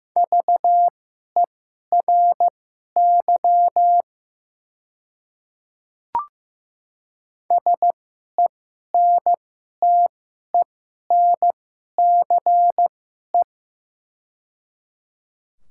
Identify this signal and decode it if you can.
Morse code, heard as a keyed tone, about 15 wpm, spaced as for 11 wpm, 700 Hz. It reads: VERY SENTENCE
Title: 2x Character Spacing - Top 200 Words - 15wpm